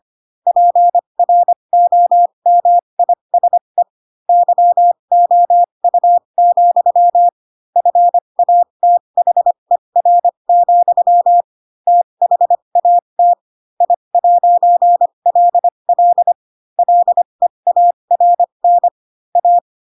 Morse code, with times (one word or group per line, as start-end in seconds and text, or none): 0.5-3.8 PROMISE
4.3-7.3 YOU,
7.8-11.4 FATHER,
11.9-13.3 THAT
13.8-16.3 I'LL
16.8-18.9 LEARN
19.4-19.6 A